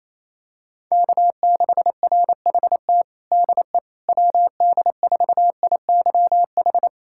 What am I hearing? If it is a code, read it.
Morse code, heard as a keyed tone, about 28 wpm, 700 Hz. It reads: K6RHT DE WD4IYH
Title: Practice QSO - Exchange 4 - Ending - 28wpm